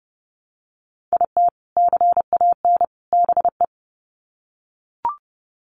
Morse code, heard as a keyed tone, about 30 wpm, 700 Hz. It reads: IT CAN BE
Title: Sets of 3 Words 30wpm